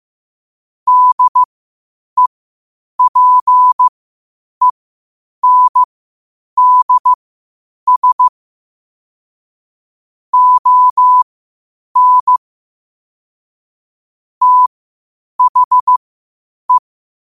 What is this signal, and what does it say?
Morse code, heard as a keyed tone, about 15 wpm, spaced as for 8 wpm, 1000 Hz. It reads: DEPENDS ON THE
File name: qotdfiles / 8wpm